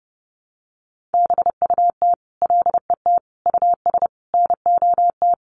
Morse code, heard as a keyed tone, about 30 wpm, 700 Hz. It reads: BUT LET US NOT
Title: Sentences from Top 1500 Words 30wpm